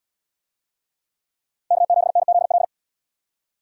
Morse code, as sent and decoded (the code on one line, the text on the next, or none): -.- --... -. --.. .-..
K7NZL